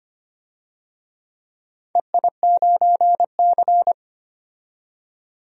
Morse code, heard as a keyed tone, about 25 wpm, 700 Hz. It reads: EI9C
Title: DXCC-100 Call Signs - 4 Characters - 25wpm